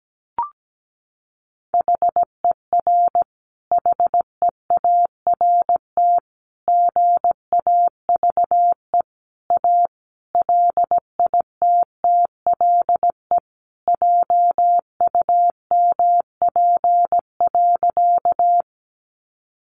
Morse code, heard as a keyed tone, about 17 wpm, 700 Hz. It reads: HER HEART GAVE A LITTLE JUMP.
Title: Sentences from Top 1300 Words 17wpm